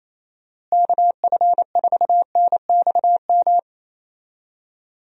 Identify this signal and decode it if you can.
Morse code, heard as a keyed tone, about 27 wpm, 700 Hz. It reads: KF4NXM
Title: USA Call Signs - 6 Characters - Repeated 2x - 28wpm